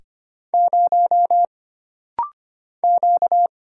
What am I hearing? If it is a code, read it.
Morse code, heard as a keyed tone, about 25 wpm, 700 Hz. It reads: 0 Q
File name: Single Character (Letter:Number:Punctuation:Prosign) - Rapid-Fire - 25wpm